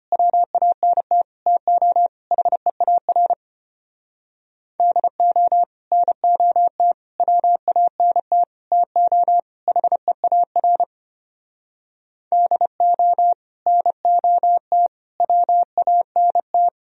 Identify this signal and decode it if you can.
Morse code, heard as a keyed tone, about 32 wpm, 700 Hz. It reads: WANT TO HEAR DO NOT WANT TO HEAR DO NOT WANT